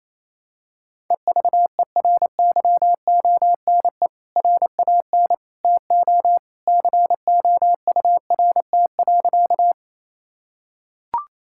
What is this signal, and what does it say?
Morse code, heard as a keyed tone, about 28 wpm, 700 Hz. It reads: EVERYONE RAN TO COURT.
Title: Sentences from Top 1600 Words 28wpm